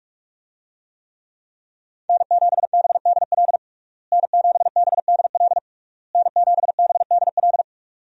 Morse code, heard as a keyed tone, about 46 wpm, 700 Hz. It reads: N7BDL N7BDL N7BDL
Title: USA Call Signs - 5 Characters - Repeated 3x - 45wpm